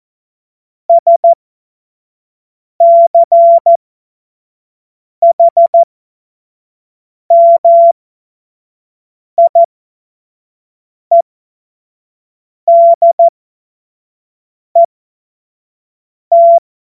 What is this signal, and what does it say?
Morse code, heard as a keyed tone, about 14 wpm, spaced as for 5 wpm, 680 Hz. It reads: SCHMIEDET
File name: Morsekurs / Witze / Witz 21. März 2023 25 F70